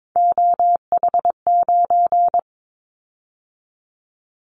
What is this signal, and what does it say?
Morse code, heard as a keyed tone, about 22 wpm, 700 Hz. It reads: OH9
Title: Callsign Prefixes for Top 50 DXCC - 22wpm